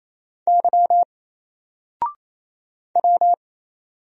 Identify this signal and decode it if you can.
Morse code, heard as a keyed tone, about 27 wpm, 700 Hz. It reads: Y W
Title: Single Letter-Number Alt 28wpm